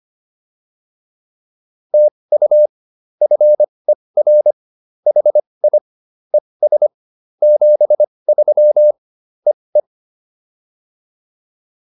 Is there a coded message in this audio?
Yes: Morse code from 1.9 to 9.8 s, heard as a keyed tone, about 25 wpm, spaced as for 20 wpm, 590 Hz. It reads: TU FER HI ES 73 EE